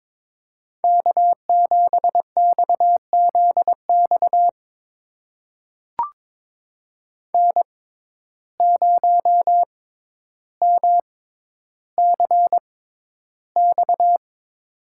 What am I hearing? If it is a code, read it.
Morse code, heard as a keyed tone, about 22 wpm, spaced as for 11 wpm, 700 Hz. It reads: K7XZX N0MCX
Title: USA Call Signs - 5 Characters - 6x Character Spacing - 22wpm